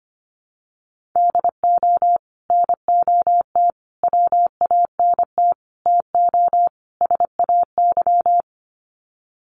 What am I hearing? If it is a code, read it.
Morse code, heard as a keyed tone, about 25 wpm, 700 Hz. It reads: DO NOT WANT TO SAY
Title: Sets of 5 Words Encore 25wpm